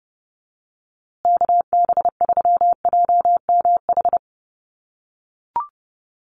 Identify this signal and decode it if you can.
Morse code, heard as a keyed tone, about 30 wpm, 700 Hz. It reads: KB3JMH
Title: USA Call Signs Repeated 3x 30wpm